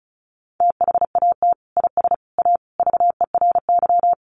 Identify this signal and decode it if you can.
Morse code, heard as a keyed tone, about 34 wpm, 700 Hz. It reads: THAT IS A VERY
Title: Sentences from Top 1200 Words 35wpm